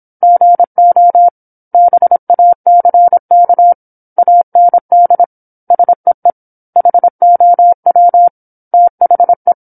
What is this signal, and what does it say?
Morse code, heard as a keyed tone, about 27 wpm, 700 Hz. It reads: GO BACK AND SEE HOW THE
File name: 08-Chapter-08 26 WPM